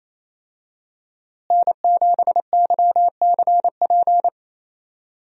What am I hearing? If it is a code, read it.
Morse code, heard as a keyed tone, about 28 wpm, 700 Hz. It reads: N7YCP
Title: USA Call Signs - 5 Characters - Repeated 2x - 28wpm